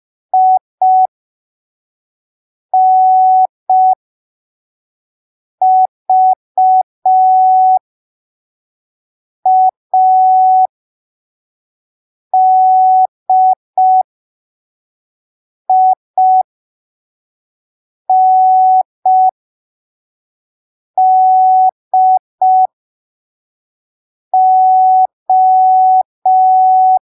Morse code, heard as a keyed tone, about 5 wpm, spaced as for 3 wpm, 750 Hz. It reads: INVADINDO
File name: arquivos / texto / texto 25 05ppm